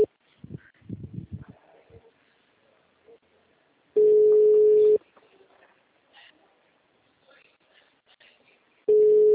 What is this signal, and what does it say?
Ringback tone (Europe, 425 Hz).